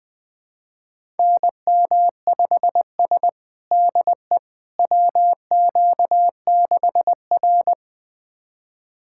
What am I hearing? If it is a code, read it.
Morse code, heard as a keyed tone, about 20 wpm, 700 Hz. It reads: NM5S DE WQ6R
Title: USA Callsign DE Callsign - 4 Characters - 20wpm